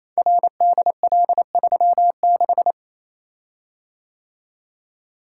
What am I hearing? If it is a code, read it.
Morse code, heard as a keyed tone, about 29 wpm, 700 Hz. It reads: RDL36
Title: Code Groups - 5 Letter:Numbers - 28wpm